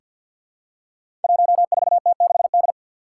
Morse code, heard as a keyed tone, about 50 wpm, 700 Hz. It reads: 14T6D
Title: Five Challenging Letter-Number Combinations 50wpm